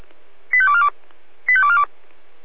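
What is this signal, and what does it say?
Five-tone selective call (ZVEI), heard as quick runs of five tones, tones of about 70 ms. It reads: 85242 85242
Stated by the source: Five-tone ZVEI selective call 85230, received on SDR.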